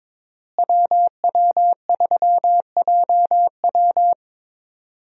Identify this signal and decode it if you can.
Morse code, heard as a keyed tone, about 22 wpm, 690 Hz. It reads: WW3JW
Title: USA Call Signs - 5 Characters - Repeated 3x - 22wpm